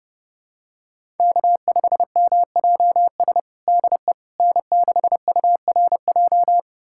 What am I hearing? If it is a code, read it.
Morse code, heard as a keyed tone, about 32 wpm, 700 Hz. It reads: K5MJS DE N6URJ
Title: USA Callsign DE Callsign - 5 Characters - 30wpm